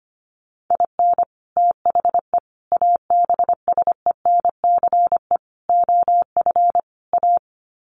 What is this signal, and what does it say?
Morse code, heard as a keyed tone, about 25 wpm, 700 Hz. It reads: IN THE ABSENCE OF A
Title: Sets of 5 Words Encore 25wpm